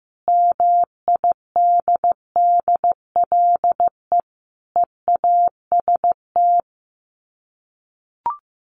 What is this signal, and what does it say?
Morse code, heard as a keyed tone, about 15 wpm, 700 Hz. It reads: MIDDLE EAST
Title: Sets of 4 Words 15wpm